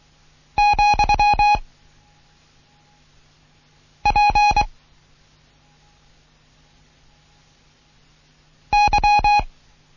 Morse code, heard as a keyed tone, about 23 wpm, 830 Hz. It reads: , P Y
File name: Morsen / Kurs A 16